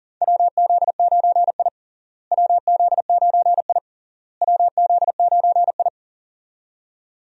Morse code, heard as a keyed tone, about 40 wpm, 700 Hz. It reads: WZ9I WZ9I WZ9I
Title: USA Call Signs - 4 Characters - Repeated 3x - 40wpm